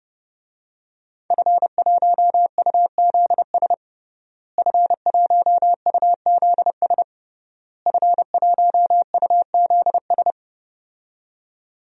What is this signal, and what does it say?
Morse code, heard as a keyed tone, about 30 wpm, 700 Hz. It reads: F1UZS F1UZS F1UZS
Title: DXCC-100 Call Signs - 5 Characters - Repeated 3x - 30wpm